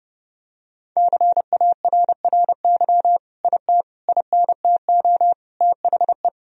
Morse code, heard as a keyed tone, about 30 wpm, 700 Hz. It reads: CARRY IT INTO THE
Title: Sentences from Top 1000 Words - Review Part 1 - 30wpm